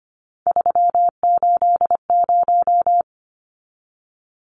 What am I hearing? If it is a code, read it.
Morse code, heard as a keyed tone, about 25 wpm, 690 Hz. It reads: 380
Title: Numbers - 3-Digit 25wpm